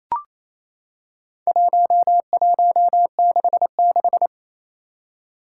Morse code, heard as a keyed tone, about 28 wpm, 700 Hz. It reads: 1166
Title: Four Challenging Letter-Number Combinations 28wpm